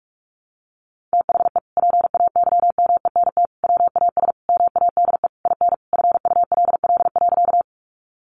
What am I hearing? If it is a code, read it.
Morse code, heard as a keyed tone, about 44 wpm, 700 Hz. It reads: THE PAYMENT WAS MADE IN FULL.
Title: Sentences from Top 1200 Words 45wpm